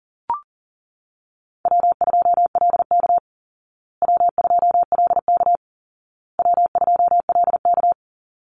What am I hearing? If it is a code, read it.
Morse code, heard as a keyed tone, about 40 wpm, 700 Hz. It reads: W2LK W2LK W2LK
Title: Easy Call Signs Repeated 3x - 40wpm